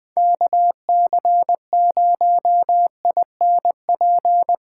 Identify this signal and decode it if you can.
Morse code, heard as a keyed tone, about 20 wpm, 700 Hz. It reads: KC0INP
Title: K1USN SST QSO Practice - 20wpm